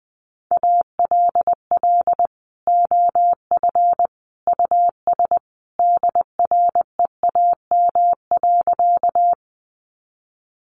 Morse code, heard as a keyed tone, about 20 wpm, 700 Hz. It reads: ALL OF US DREAM.
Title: Sentences from Top 1500 Words 20wpm